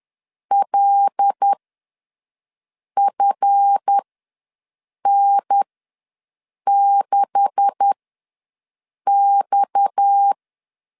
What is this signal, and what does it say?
Morse code, heard as a keyed tone, about 10 wpm, spaced as for 5 wpm, 800 Hz. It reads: LFN6X
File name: LEZIONI / 10 SIGNS / 10A SLASH ws 10wpm 4121